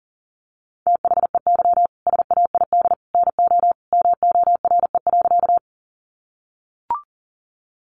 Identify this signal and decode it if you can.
Morse code, heard as a keyed tone, about 37 wpm, 700 Hz. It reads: THEY SAID NO MORE.
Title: Sentences from Top 100 Words 40wpm